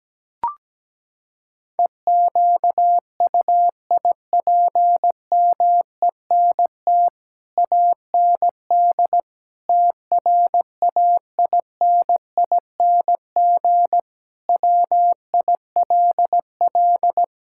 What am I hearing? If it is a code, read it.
Morse code, heard as a keyed tone, about 17 wpm, 700 Hz. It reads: EQUIPMENT AND TRAINING WILL
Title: Sentences from Top 1000 Words - Review Part 5 - 17wpm